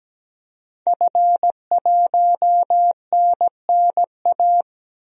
Morse code, heard as a keyed tone, about 17 wpm, 700 Hz. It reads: F1NNA
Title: DXCC-100 Call Signs - 5 Characters - Repeated 3x - 17wpm